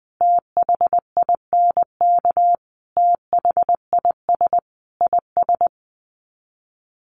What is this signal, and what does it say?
Morse code, heard as a keyed tone, about 20 wpm, 700 Hz. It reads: THINK THIS IS
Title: Sets of 5 Words Encore2 20wpm